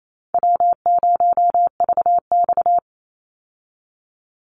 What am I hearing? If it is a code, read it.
Morse code, heard as a keyed tone, about 28 wpm, 700 Hz. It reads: W0VX
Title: Call Signs Repeated 3x - 28wpm